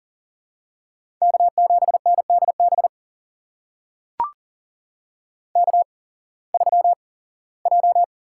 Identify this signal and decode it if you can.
Morse code, heard as a keyed tone, about 39 wpm, spaced as for 21 wpm, 700 Hz. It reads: K7NDB K3J
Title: USA Call Signs - 5 Characters - 8x Character Spacing - 40wpm